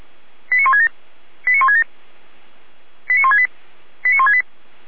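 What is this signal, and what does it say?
Five-tone selective call (ZVEI), heard as quick runs of five tones, tones of about 70 ms. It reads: 89167 89167 89168 89168
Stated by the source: ZVEI five-tone call received on an SDR, fire brigade Orlamünde.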